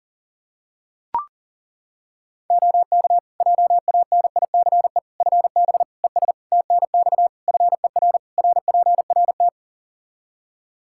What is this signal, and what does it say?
Morse code, heard as a keyed tone, about 40 wpm, 700 Hz. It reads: OK JANICE FB ES TNX FER RPRT
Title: K1USN SST QSO Practice 40wpm